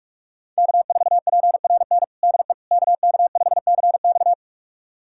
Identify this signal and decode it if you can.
Morse code, heard as a keyed tone, about 41 wpm, 700 Hz. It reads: K4PRN DE KK5CX
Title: USA Callsign DE Callsign - 5 Characters - 45wpm